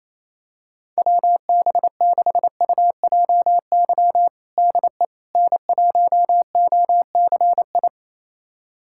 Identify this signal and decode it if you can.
Morse code, heard as a keyed tone, about 27 wpm, 700 Hz. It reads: WB6UJY DE N1OCI
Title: USA Callsign DE Callsign 28wpm